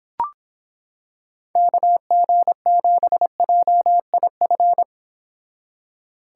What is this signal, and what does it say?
Morse code, heard as a keyed tone, about 26 wpm, 700 Hz. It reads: KG7JIF